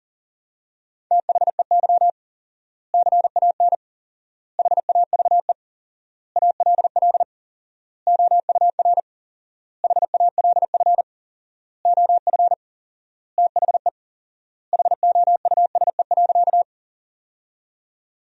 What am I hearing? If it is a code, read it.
Morse code, heard as a keyed tone, about 40 wpm, 700 Hz. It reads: THEY CAN HAVE ALL OUR HALF OF THE HOUSE.
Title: 4x Word-Spacing - Sentences from Top 400 Words - 40wpm